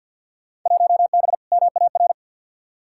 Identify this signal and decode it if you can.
Morse code, heard as a keyed tone, about 51 wpm, 700 Hz. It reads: 1B MAR
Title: Field Day Exchanges Encore - 50wpm